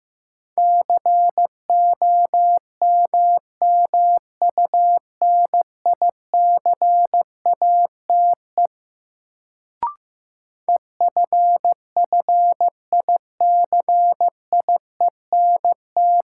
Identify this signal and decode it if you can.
Morse code, heard as a keyed tone, about 15 wpm, 700 Hz. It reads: COMMUNICATE EFFICIENT
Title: Sentences from Top 1300 Words - New Word Drill - 15wpm